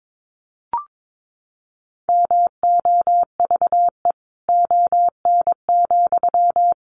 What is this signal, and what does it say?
Morse code, heard as a keyed tone, about 22 wpm, 700 Hz. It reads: MOVE ON,
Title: Sentences from Top 500 Words 22wpm